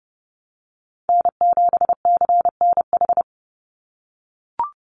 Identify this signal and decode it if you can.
Morse code, heard as a keyed tone, about 30 wpm, 700 Hz. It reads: N7CNH
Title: NA SOTA Chaser Callsigns 30wpm